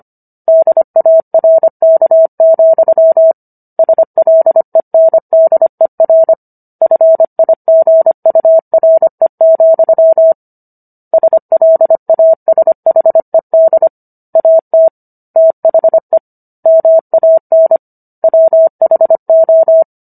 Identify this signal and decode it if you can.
Morse code, heard as a keyed tone, about 25 wpm, 650 Hz. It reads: DARK, SLENDER FIGURE, SLASHED AT THE MAN WHO